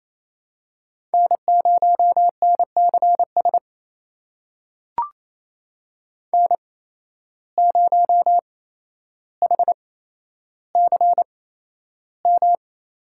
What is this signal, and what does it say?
Morse code, heard as a keyed tone, about 28 wpm, spaced as for 12 wpm, 700 Hz. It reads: N0NCS N0HCM
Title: USA Call Signs - 5 Characters - 8x Character Spacing - 28wpm